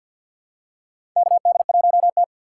Morse code, heard as a keyed tone, about 50 wpm, 700 Hz. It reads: KD1T
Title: USA Call Signs - 4 Characters - 3x Character Spacing - 50wpm